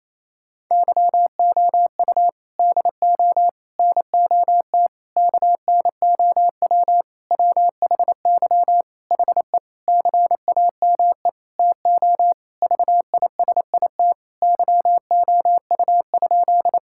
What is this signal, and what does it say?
Morse code, heard as a keyed tone, about 28 wpm, 700 Hz. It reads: YOU DO NOT KNOW WHY HE CAME TO VISIT YOU?
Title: Sentences from Top 1000 Words - Review Part 6 - 28wpm